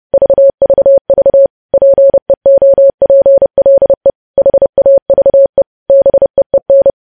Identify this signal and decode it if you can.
Morse code, heard as a keyed tone, about 30 wpm, 550 Hz. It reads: VVV PEOPLE HAVE BEEN
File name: podcast / News headlines-1x-30x30